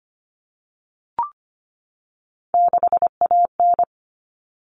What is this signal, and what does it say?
Morse code, heard as a keyed tone, about 25 wpm, 700 Hz. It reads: 6AN
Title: All Cut Numbers - 3-Digit 25wpm